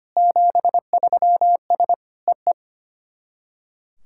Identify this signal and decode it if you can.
Morse code, heard as a keyed tone, about 25 wpm, 700 Hz. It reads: 73S EE
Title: SOTA QSO Practice 25wpm